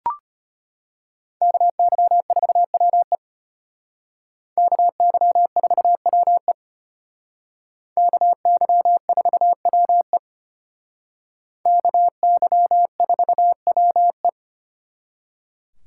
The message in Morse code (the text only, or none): KY4WE KY4WE KY4WE KY4WE